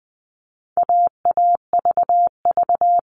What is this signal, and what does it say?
Morse code, heard as a keyed tone, about 20 wpm, 700 Hz. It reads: AAVV
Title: All Cut Numbers - 4-Digit 20wpm